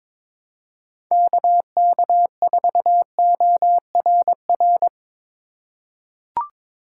Morse code, heard as a keyed tone, about 22 wpm, 700 Hz. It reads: KK4ORR
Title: USA Call Signs - 6 Characters - 3x Character Spacing - 22wpm